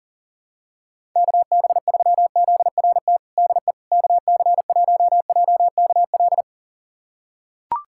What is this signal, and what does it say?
Morse code, heard as a keyed tone, about 40 wpm, 700 Hz. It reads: KB3ZRT DE KC1JKL
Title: USA Callsign DE Callsign 40wpm